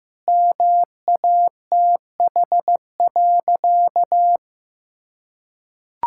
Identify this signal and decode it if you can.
Morse code, heard as a keyed tone, about 15 wpm, 700 Hz. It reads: MATH.
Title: Sentences from Top 1000 Words - Review Part 6 - 15wpm